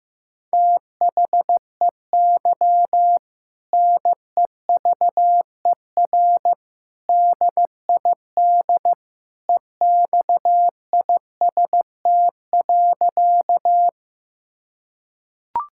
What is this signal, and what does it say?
Morse code, heard as a keyed tone, about 15 wpm, 700 Hz. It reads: THEY NEVER DID EXIST.
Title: Sentences from Top 1000 Words - Review Part 5 - 15wpm